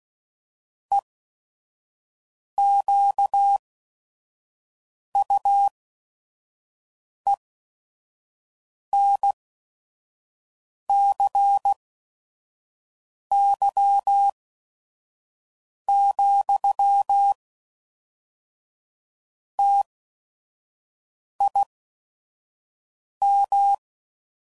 Morse code, heard as a keyed tone, about 16 wpm, spaced as for 5 wpm, 800 Hz. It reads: EQUENCY, TIM
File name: CW Code Groups @5WPM / Technical Subjects / XC